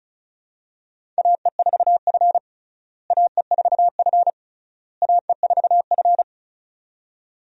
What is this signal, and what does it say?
Morse code, heard as a keyed tone, about 36 wpm, 700 Hz. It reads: AE4F AE4F AE4F
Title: USA Call Signs - 4 Characters - Repeated 3x - 35wpm